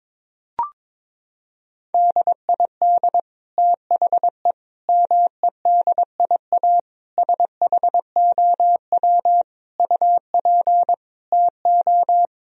Morse code, heard as a keyed tone, about 22 wpm, 700 Hz. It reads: DID THE MEDIA SHOW UP TO
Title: Sentences from Top 1000 Words - Review Part 4 - 22wpm